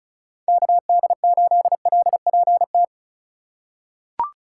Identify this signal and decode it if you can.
Morse code, heard as a keyed tone, about 34 wpm, 700 Hz. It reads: KD8LPT
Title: USA Call Signs - 6 Characters - 4x Character Spacing - 35wpm